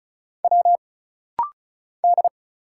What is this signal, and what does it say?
Morse code, heard as a keyed tone, about 34 wpm, 700 Hz. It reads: W D
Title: Single Letters - Rapid-Fire 35wpm